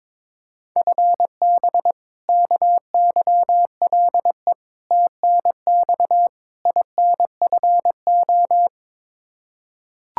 Morse code, heard as a keyed tone, about 22 wpm, 700 Hz. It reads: FB KYLE TNX INFO
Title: Practice QSO - Exchange 3 - More Info - 22wpm